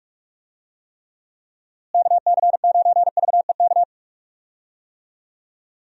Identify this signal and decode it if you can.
Morse code, heard as a keyed tone, about 46 wpm, 700 Hz. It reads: KC9VEK
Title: USA Call Signs - 6 Characters - Repeated 3x - 45wpm